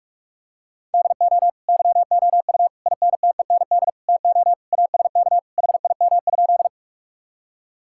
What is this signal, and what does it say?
Morse code, heard as a keyed tone, about 46 wpm, 700 Hz. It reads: DO YOU INTEND TO ASK HIM?
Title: Sentences from Top 1600 Words 45wpm